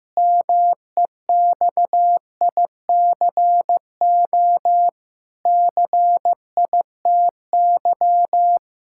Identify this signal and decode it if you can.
Morse code, heard as a keyed tone, about 15 wpm, 700 Hz. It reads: MEXICO CITY
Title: Pangram Challenge v2 - 15wpm